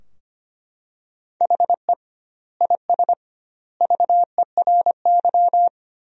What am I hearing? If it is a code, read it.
Morse code, heard as a keyed tone, about 25 wpm, 700 Hz. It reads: HE IS VERY
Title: 2x Word-Spacing - Sentences from Top 400 Words - 25wpm